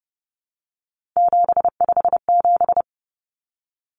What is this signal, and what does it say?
Morse code, heard as a keyed tone, about 30 wpm, 700 Hz. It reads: 757
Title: Common Cut Numbers - 3-Digit 30wpm